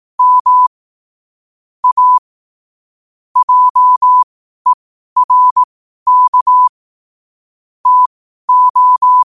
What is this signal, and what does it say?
Morse code, heard as a keyed tone, about 18 wpm, spaced as for 12 wpm, 1000 Hz. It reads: M A JERK TO
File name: qotdfiles / 12wpm